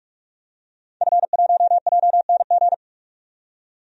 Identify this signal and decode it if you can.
Morse code, heard as a keyed tone, about 44 wpm, 700 Hz. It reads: F1JNG
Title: DXCC-100 Call Signs - 5 Characters - 45wpm